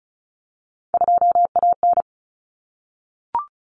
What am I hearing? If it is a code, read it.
Morse code, heard as a keyed tone, about 34 wpm, 700 Hz. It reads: 2AN
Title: Common Cut Numbers - 3-Digit 35wpm